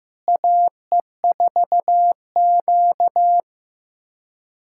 Morse code, heard as a keyed tone, about 15 wpm, 700 Hz. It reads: AE4Q
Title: USA Callsign DE Callsign - 4 Characters - 15wpm